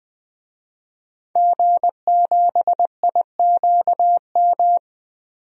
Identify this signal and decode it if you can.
Morse code, heard as a keyed tone, about 20 wpm, 700 Hz. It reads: G7IQM
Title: DXCC-100 Call Signs - 5 Characters - 20wpm